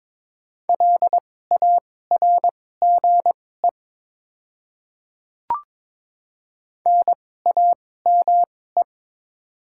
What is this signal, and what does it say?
Morse code, heard as a keyed tone, about 22 wpm, spaced as for 16 wpm, 700 Hz. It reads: LARGE NAME
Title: 2x Character Spacing - Top 200 Words - 22wpm